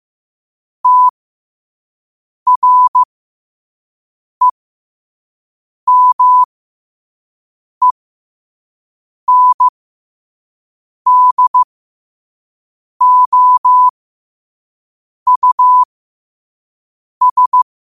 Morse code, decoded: TREMENDOUS

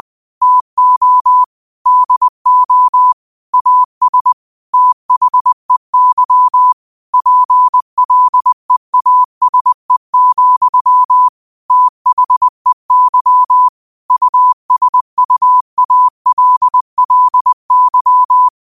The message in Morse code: TO DO AS THEY PLEASE, THEY USUALLY

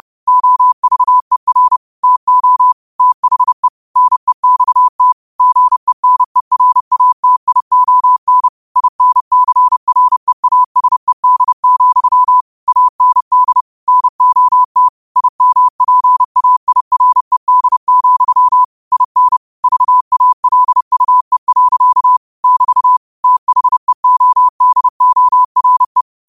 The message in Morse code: OVER TO THE NEXT GENERATION INCREASED, AND NOT IMPAIRED, IN VALUE. <BT> THEODORE

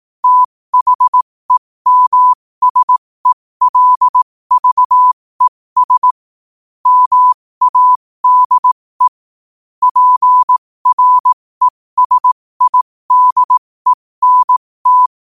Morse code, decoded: THEMSELVES MADE PRESIDENT

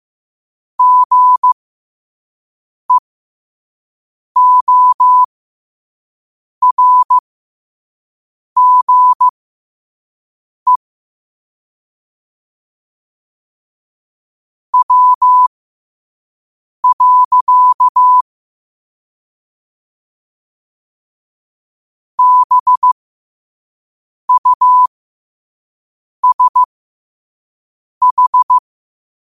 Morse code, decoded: GEORGE W. BUSH

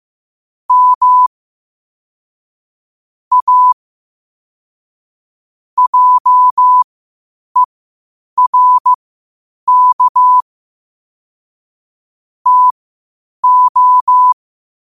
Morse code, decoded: M A JERK TO

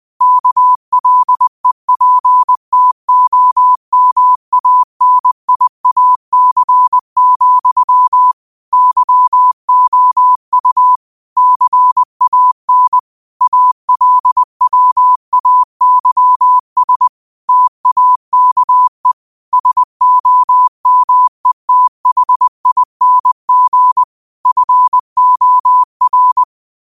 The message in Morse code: KLEPTOMANIAC, YOU CAN ALWAYS TAKE SOMETHING FOR

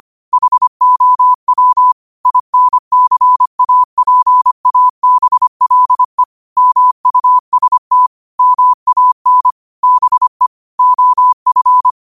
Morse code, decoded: HOW INCAPABLE MUST MAN BE OF

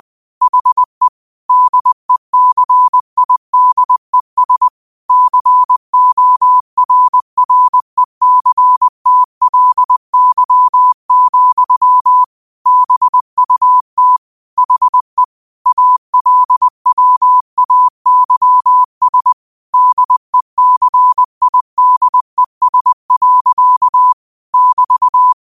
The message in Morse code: HE DECIDES CORRECTLY, BUT HE ALWAYS DECIDES. <BT>

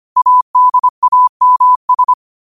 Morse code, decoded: ADAMS

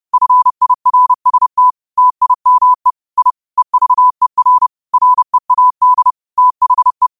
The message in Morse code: FIRST TIME I EVER READ THE

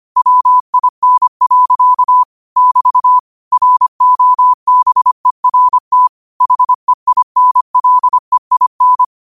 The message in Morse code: WIN. <BT> ROBERT HEINLEIN

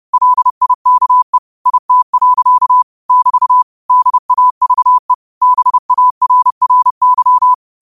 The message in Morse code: LIKE IT. <BT> DAVE BARRY

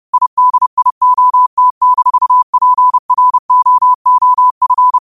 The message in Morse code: IDIOT-PROOF

